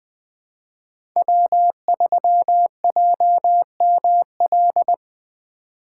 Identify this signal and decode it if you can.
Morse code, heard as a keyed tone, about 25 wpm, 700 Hz. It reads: W3JML
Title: Speed Racing - USA 1x3 Call Signs 20wpm